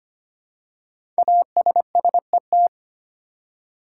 Morse code, heard as a keyed tone, about 25 wpm, 700 Hz. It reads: ASSET
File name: 5 Letter Words - First 3 Letters Given - 25wpm